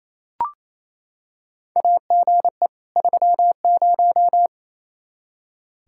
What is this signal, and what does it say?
Morse code, heard as a keyed tone, about 28 wpm, 700 Hz. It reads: AGE 30
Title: Practice QSO - Exchange 3 - More Info - 28wpm